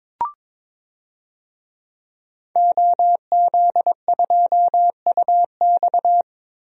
Morse code, heard as a keyed tone, about 22 wpm, 700 Hz. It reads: OZ2UX
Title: DXCC-100 Call Signs - 5 Characters - Repeated 3x - 22wpm